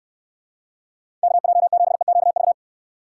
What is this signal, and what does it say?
Morse code, heard as a keyed tone, about 67 wpm, 700 Hz. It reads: KJ7EQV